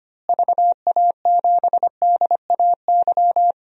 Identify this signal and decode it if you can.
Morse code, heard as a keyed tone, about 25 wpm, 700 Hz. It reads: VA7DAY
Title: DXCC-100 Call Signs - 6 Characters - Repeated 3x - 25wpm